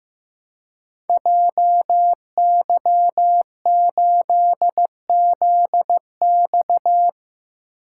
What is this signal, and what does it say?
Morse code, heard as a keyed tone, about 15 wpm, 700 Hz. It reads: JY8ZX
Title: Code Groups - 5 Letter:Numbers - 15wpm